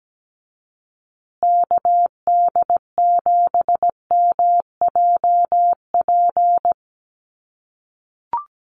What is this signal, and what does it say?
Morse code, heard as a keyed tone, about 17 wpm, 700 Hz. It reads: KD7MJP
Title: USA Call Signs Repeated 3x 17wpm